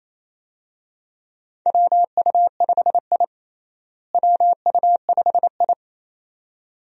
Morse code, heard as a keyed tone, about 27 wpm, 700 Hz. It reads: WU5I WU5I
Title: USA Call Signs - 4 Characters - Repeated 2x - 28wpm